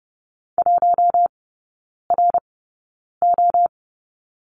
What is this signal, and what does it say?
Morse code, heard as a keyed tone, about 30 wpm, spaced as for 9 wpm, 700 Hz. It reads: 1RO
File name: course / 30:10wpm / 087 Lesson 15e - Number 5 - Three Characters 30wpm